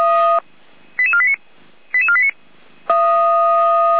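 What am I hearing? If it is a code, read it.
Five-tone selective call (ZVEI), heard as quick runs of five tones, tones of about 70 ms. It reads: 88489 88489
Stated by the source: ZVEI five-tone call 88217 received on an SDR.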